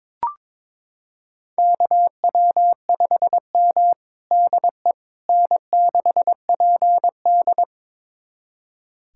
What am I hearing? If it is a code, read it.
Morse code, heard as a keyed tone, about 22 wpm, 700 Hz. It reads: KW5M DE N6PD
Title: USA Callsign DE Callsign - 4 Characters - 22wpm